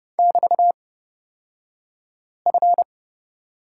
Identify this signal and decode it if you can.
Morse code, heard as a keyed tone, about 30 wpm, spaced as for 5 wpm, 700 Hz. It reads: <BT>F
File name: Lesson 38d - Letter q - Two Characters 30wpm